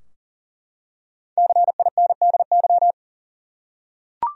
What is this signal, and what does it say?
Morse code, heard as a keyed tone, about 40 wpm, 700 Hz. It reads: CINDY
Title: USA Common Names - 40wpm